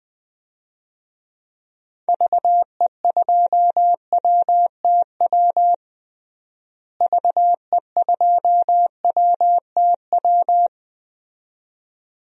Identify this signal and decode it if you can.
Morse code, heard as a keyed tone, about 21 wpm, 700 Hz. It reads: VE2WTW VE2WTW
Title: DXCC-100 Call Signs - 6 Characters - Repeated 2x - 20wpm